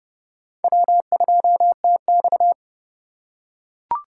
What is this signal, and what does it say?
Morse code, heard as a keyed tone, about 29 wpm, 700 Hz. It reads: W2TX
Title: USA Call Signs - 4 Characters - 2x Character Spacing - 30wpm